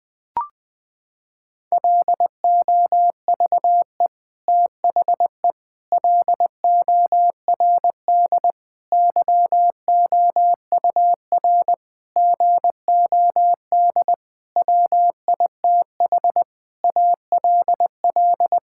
Morse code, heard as a keyed tone, about 20 wpm, 700 Hz. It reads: LOVE THE LORD YOUR GOD WITH ALL
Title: Sets of 9 Words - 20wpm